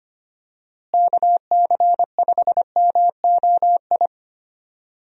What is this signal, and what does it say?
Morse code, heard as a keyed tone, about 25 wpm, 700 Hz. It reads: KC5MOI
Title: USA Call Signs - 6 Characters - Repeated 3x - 25wpm